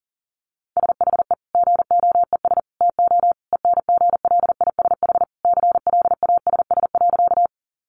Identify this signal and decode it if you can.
Morse code, heard as a keyed tone, about 40 wpm, 700 Hz. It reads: SHE GOES TO ENGLISH CLASS.
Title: Additional Easy Phrases 40wpm